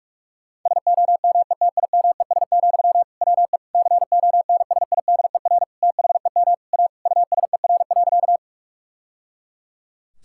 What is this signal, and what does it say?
Morse code, heard as a keyed tone, about 42 wpm, 700 Hz. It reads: SOMETIMES, WE CONSIDER THEM A USER.
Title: Sentences from Top 1000 Words - Review Part 5 - 45wpm